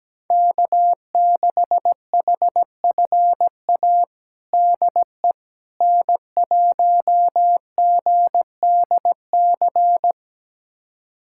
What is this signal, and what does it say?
Morse code, heard as a keyed tone, about 17 wpm, 700 Hz. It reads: K6HFA DE N1GDC
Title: USA Callsign DE Callsign - 5 Characters - 17wpm